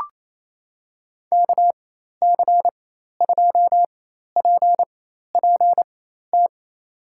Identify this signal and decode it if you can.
Morse code, heard as a keyed tone, about 27 wpm, spaced as for 13 wpm, 700 Hz. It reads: KC2PPT
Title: USA Call Signs - 6 Characters - 4x Character Spacing - 28wpm